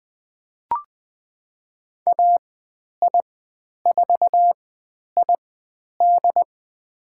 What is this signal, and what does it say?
Morse code, heard as a keyed tone, about 20 wpm, spaced as for 10 wpm, 700 Hz. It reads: AI4ID
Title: 095 Lesson 16g - Letter c - Call Signs 20wpm